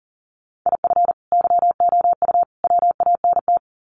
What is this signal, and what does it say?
Morse code, heard as a keyed tone, about 40 wpm, 700 Hz. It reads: IF YOU WANT
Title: Sets of 3 Words 40wpm